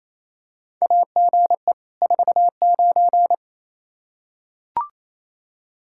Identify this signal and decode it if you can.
Morse code, heard as a keyed tone, about 28 wpm, 700 Hz. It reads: AGE 49
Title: Practice QSO - Exchange 3 - More Info - 28wpm